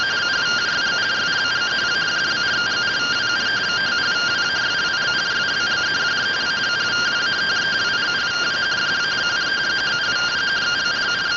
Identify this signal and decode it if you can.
Two-tone teleprinter signal, about 170 Hz shift.